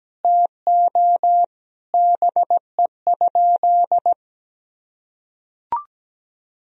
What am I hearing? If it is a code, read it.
Morse code, heard as a keyed tone, about 17 wpm, 700 Hz. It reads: TO BE?
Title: Sentences from Top 1000 Words - Review Part 2 - 17wpm